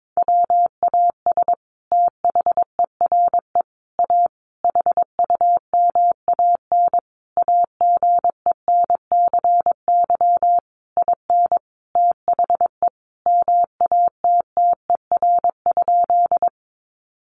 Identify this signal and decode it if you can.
Morse code, heard as a keyed tone, about 22 wpm, 700 Hz. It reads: WAS THERE A HUMAN AGENCY IN THE MATTER?
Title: Sentences from Top 1300 Words 22wpm